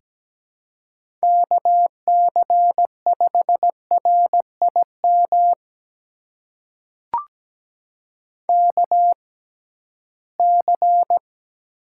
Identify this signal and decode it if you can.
Morse code, heard as a keyed tone, about 17 wpm, spaced as for 13 wpm, 700 Hz. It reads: KC5RIM KC
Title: USA Call Signs - 6 Characters - 6x Character Spacing - 17wpm